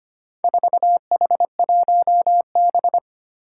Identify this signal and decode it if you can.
Morse code, heard as a keyed tone, about 25 wpm, 700 Hz. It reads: 4H1B
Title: Four Challenging Letter-Number Combinations 25wpm